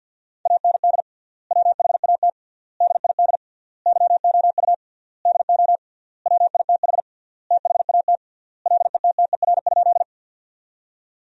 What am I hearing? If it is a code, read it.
Morse code, heard as a keyed tone, about 52 wpm, 700 Hz. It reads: AND WHAT DID YOU DO WITH THAT LETTER?
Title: 3x Word-Spacing - Sentences from Top 300 Words - 50wpm